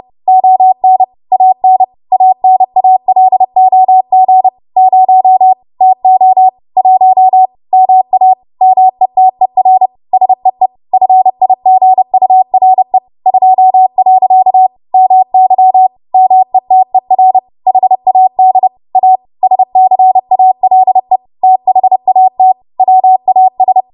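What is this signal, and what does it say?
Morse code, heard as a keyed tone, about 30 wpm, 750 Hz. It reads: ON AN ANALOG 0 TO 1 MA METER SEE FIGURE 2. MY METER HAD A SCALE THAT WAS
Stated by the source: W1AW Web Code Practice Files, 30 WPM run from ARRL; text from April 2023 QST.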